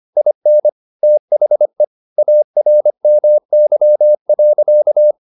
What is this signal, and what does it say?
Morse code, heard as a keyed tone, about 25 wpm, 600 Hz. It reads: IN THE ARMY.